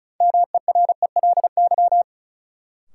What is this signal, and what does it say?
Morse code, heard as a keyed tone, about 36 wpm, 700 Hz. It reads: MERELY
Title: Sentences from Top 1400 Words - New Word Drill Alt - 35wpm